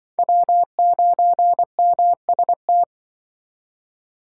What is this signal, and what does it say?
Morse code, heard as a keyed tone, about 23 wpm, 700 Hz. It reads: W9MST
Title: Speed Racing - USA 1x3 Call Signs 20wpm